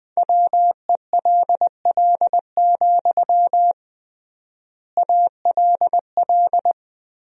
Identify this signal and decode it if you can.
Morse code, heard as a keyed tone, about 20 wpm, 700 Hz. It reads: WELL, ALL
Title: 3x Word-Spacing - Sentences from Top 200 Words - 20wpm